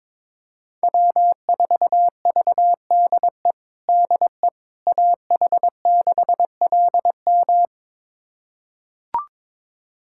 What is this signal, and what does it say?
Morse code, heard as a keyed tone, about 22 wpm, 700 Hz. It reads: W4VDE DE AH6LM
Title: USA Callsign DE Callsign - 5 Characters - 22wpm